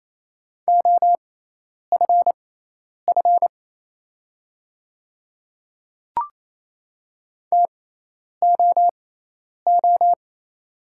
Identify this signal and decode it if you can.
Morse code, heard as a keyed tone, about 27 wpm, spaced as for 10 wpm, 690 Hz. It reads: OFF TOO
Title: 6x Character Spacing - Top 300 Words - 28wpm